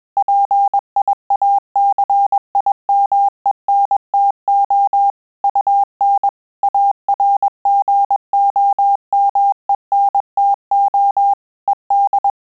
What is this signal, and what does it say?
Morse code, heard as a keyed tone, about 21 wpm, 800 Hz. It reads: PIACIMENTO UN ARGOMENTO ED
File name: Telegraphy-group / breve testo midi